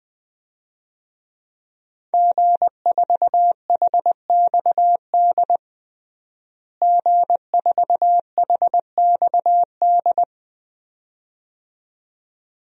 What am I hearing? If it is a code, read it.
Morse code, heard as a keyed tone, about 20 wpm, 700 Hz. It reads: G4HXD G4HXD
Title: DXCC-100 Call Signs - 5 Characters - Repeated 2x - 20wpm